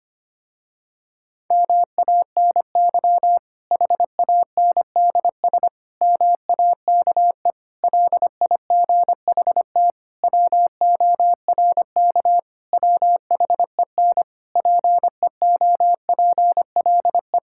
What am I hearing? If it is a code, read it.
Morse code, heard as a keyed tone, about 25 wpm, 700 Hz. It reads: MANY HANDS MAKE LIGHT WORK WHEN PEOPLE
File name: practice / 25wpm / Difficult Sentences with Hints - 25wpm